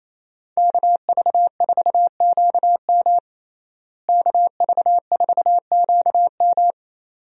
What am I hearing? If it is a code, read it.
Morse code, heard as a keyed tone, about 28 wpm, 700 Hz. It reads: KV4QM KV4QM
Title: USA Call Signs - 5 Characters - Repeated 2x - 28wpm